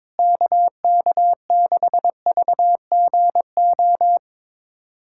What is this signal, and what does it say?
Morse code, heard as a keyed tone, about 22 wpm, 700 Hz. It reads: KK6VGO
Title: USA Callsign DE Callsign - 6 Characters - 22wpm